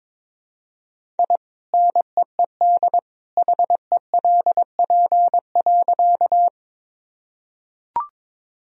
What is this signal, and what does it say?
Morse code, heard as a keyed tone, about 22 wpm, 700 Hz. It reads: I NEED HELP.
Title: Sentences from Top 1000 Words - Review Part 1 - 22wpm